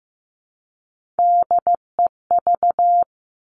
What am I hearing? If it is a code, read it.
Morse code, heard as a keyed tone, about 15 wpm, 700 Hz. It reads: DEV
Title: Three Challenging Letter-Number Combinations 15wpm